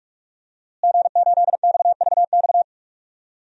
Morse code, heard as a keyed tone, about 46 wpm, 700 Hz. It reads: G8XVX
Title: DXCC-100 Call Signs - 5 Characters - Repeated 2x - 45wpm